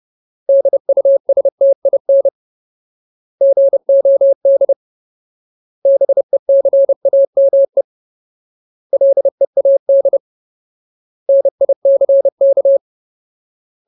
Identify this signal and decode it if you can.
Morse code, heard as a keyed tone, about 30 wpm, 550 Hz. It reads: DUSTIN GOD BECAME LEAD NICK
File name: Ultimate Word List / 30WPM Ultimate